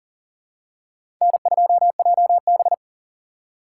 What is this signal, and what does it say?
Morse code, heard as a keyed tone, about 40 wpm, 700 Hz. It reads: N2JB
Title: USA Call Signs - 4 Characters - 3x Character Spacing - 40wpm